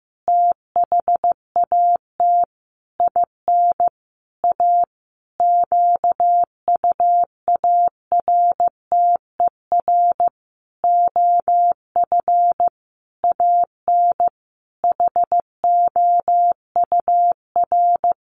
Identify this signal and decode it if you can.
Morse code, heard as a keyed tone, about 15 wpm, 700 Hz. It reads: THAT IN A QUARTER OF AN HOUR
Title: Sentences from Top 1300 Words 15wpm